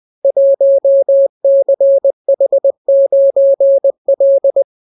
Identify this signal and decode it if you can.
Morse code, heard as a keyed tone, about 20 wpm, 550 Hz. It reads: 1CH9L